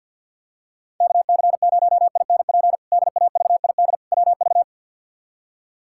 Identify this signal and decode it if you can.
Morse code, heard as a keyed tone, about 47 wpm, 700 Hz. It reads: KC0INP DAVID WV
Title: K1USN SST QSO Practice - 50wpm